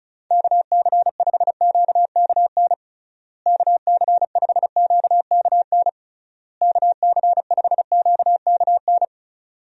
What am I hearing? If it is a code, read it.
Morse code, heard as a keyed tone, about 34 wpm, 700 Hz. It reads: KC5QKN KC5QKN KC5QKN
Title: USA Call Signs - 6 Characters - Repeated 3x - 35wpm